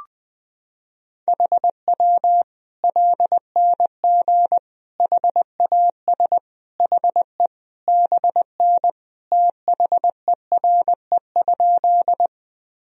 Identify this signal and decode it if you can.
Morse code, heard as a keyed tone, about 20 wpm, 700 Hz. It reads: HW LNG HAS HE BN THERE?
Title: Sentences from Top 100 Words and Abbreviations 20wpm